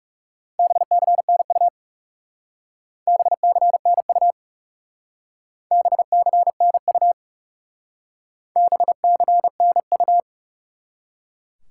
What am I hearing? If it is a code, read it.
Morse code, heard as a keyed tone, about 37 wpm, 700 Hz. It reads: BCNU BCNU BCNU BCNU
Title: Speed-Racing Top 90 QSO Elements 30wpm